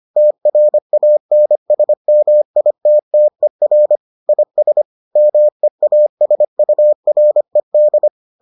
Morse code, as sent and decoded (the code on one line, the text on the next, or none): - .-. .- -. ... -- .. - - . .-. / .. ... / -- . .- ... ..- .-. -..
TRANSMITTER IS MEASURED